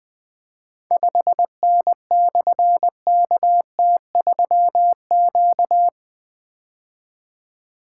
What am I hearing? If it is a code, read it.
Morse code, heard as a keyed tone, about 20 wpm, 700 Hz. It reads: 5N/KT3Q
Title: Difficult Call Signs 20wpm